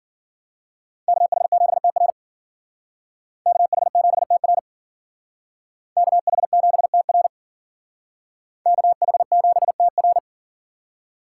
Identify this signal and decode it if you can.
Morse code, heard as a keyed tone, about 51 wpm, 700 Hz. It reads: KH7TR KH7TR KH7TR KH7TR